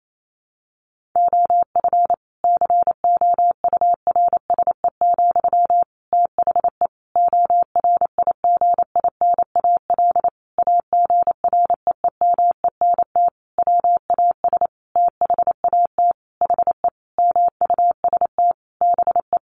Morse code, heard as a keyed tone, about 28 wpm, 700 Hz. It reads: OF COURSE, THE ORIGINAL AGREEMENT WAS THAT HE MUST BE
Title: Sentences from Top 1500 Words 28wpm